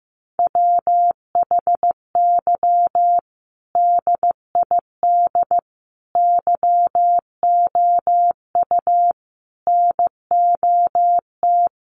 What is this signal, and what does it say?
Morse code, heard as a keyed tone, about 15 wpm, 700 Hz. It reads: WHY DID YOU NOT
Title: Sentences from Top 1300 Words 15wpm